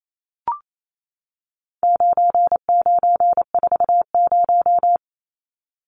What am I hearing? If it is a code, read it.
Morse code, heard as a keyed tone, about 28 wpm, 700 Hz. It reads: 9940
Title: Numbers - 4-Digit 28wpm